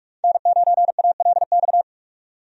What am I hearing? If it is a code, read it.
Morse code, heard as a keyed tone, about 46 wpm, 700 Hz. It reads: N9ALX
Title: USA Call Signs - 5 Characters - 45wpm